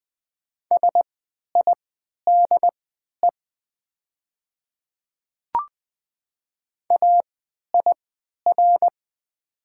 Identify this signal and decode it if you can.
Morse code, heard as a keyed tone, about 20 wpm, spaced as for 11 wpm, 700 Hz. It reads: SIDE AIR
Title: 3x Character Spacing - Top 300 Words - 20wpm